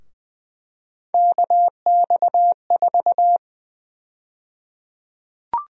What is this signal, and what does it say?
Morse code, heard as a keyed tone, about 21 wpm, 700 Hz. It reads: KX4
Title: USA Call Sign Prefixes - 20wpm